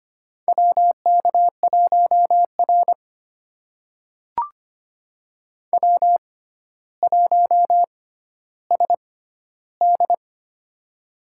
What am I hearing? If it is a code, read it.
Morse code, heard as a keyed tone, about 25 wpm, spaced as for 13 wpm, 700 Hz. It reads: WK1R W1SD
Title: USA Call Signs - 6x Character Spacing - 25wpm